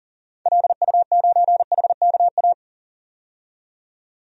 Morse code, heard as a keyed tone, about 39 wpm, 700 Hz. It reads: LU9HKA
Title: DXCC-100 Call Signs - 6 Characters - 40wpm